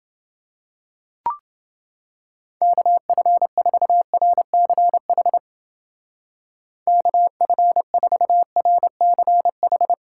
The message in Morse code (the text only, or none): KF4RCH KF4RCH